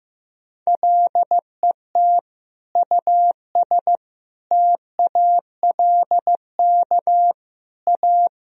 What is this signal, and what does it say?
Morse code, heard as a keyed tone, about 15 wpm, 700 Hz. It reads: LET US TALK A
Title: Sentences from Top 400 Words and Abbreviations 15wpm